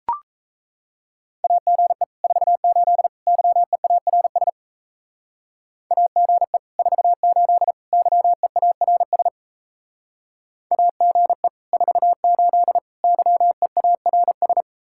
Morse code, heard as a keyed tone, about 33 wpm, 700 Hz. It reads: AGE 48 YEARS AGE 48 YEARS AGE 48 YEARS
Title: Speed-Racing Common QSO Phrases Encore - 28wpm